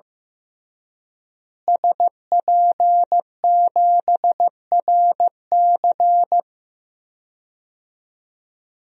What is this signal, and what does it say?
Morse code, heard as a keyed tone, about 15 wpm, 700 Hz. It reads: SP7RC
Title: DXCC-100 Call Signs - 5 Characters - Repeated 2x - 15wpm